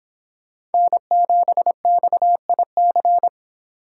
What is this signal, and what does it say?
Morse code, heard as a keyed tone, about 28 wpm, 700 Hz. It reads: N7XIC